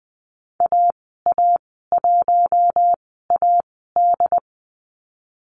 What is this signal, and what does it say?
Morse code, heard as a keyed tone, about 20 wpm, spaced as for 15 wpm, 700 Hz. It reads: AA1AD
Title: 088 Lesson 15g - Number 5 - Call Signs 20wpm